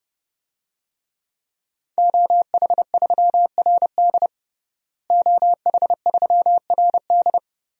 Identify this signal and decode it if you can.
Morse code, heard as a keyed tone, about 30 wpm, 700 Hz. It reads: OH3RD OH3RD
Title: DXCC-100 Call Signs - 5 Characters - Repeated 3x - 30wpm